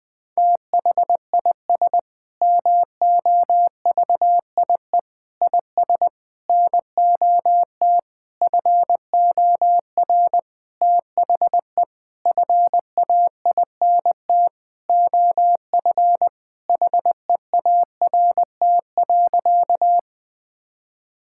Morse code, heard as a keyed tone, about 21 wpm, 700 Hz. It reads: THIS MOVIE IS NOT FOR THE FAINT OF HEART.
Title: Sentences with 7-Word Phrases - 20wpm